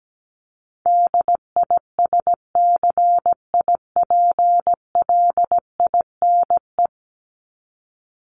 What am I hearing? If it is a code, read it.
Morse code, heard as a keyed tone, about 17 wpm, 700 Hz. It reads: DISCIPLINE
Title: Top 751-1000 Words 17wpm